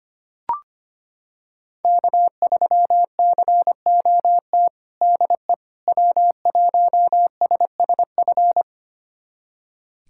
Morse code, heard as a keyed tone, about 25 wpm, 700 Hz. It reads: K3COT DE W1SSF
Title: USA Callsign DE Callsign 25wpm